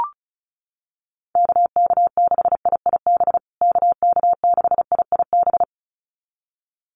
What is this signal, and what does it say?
Morse code, heard as a keyed tone, about 34 wpm, 700 Hz. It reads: KK6IIB KK6IIB
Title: USA Call Signs Repeated 2x 35wpm